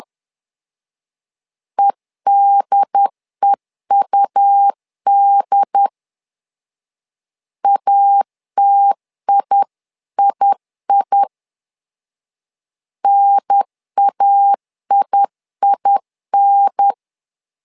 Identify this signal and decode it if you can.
Morse code, heard as a keyed tone, about 10 wpm, 800 Hz. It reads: EDEUD ATIII NAIIN
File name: LEZIONI / 2 UHDN / 2 UHDN 10wpm 1904